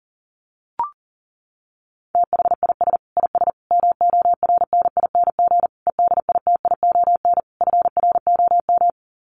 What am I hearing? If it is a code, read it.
Morse code, heard as a keyed tone, about 38 wpm, 700 Hz. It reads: THIS IS MORNING EDITION FROM
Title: Sets of 5 Words Encore2 40wpm